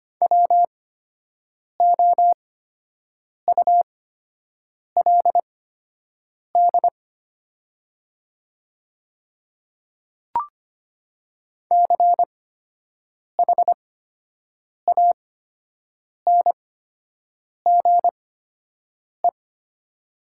Morse code, heard as a keyed tone, about 25 wpm, spaced as for 7 wpm, 700 Hz. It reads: WOULD CHANGE
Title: 8x Character Spacing - Top 200 Words - 25wpm